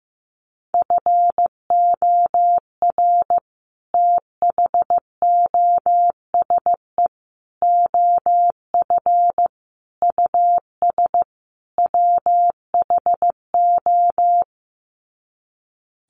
Morse code, heard as a keyed tone, about 15 wpm, 700 Hz. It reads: FOR THOSE OF US WHO
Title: Sets of 5 Words Encore 15wpm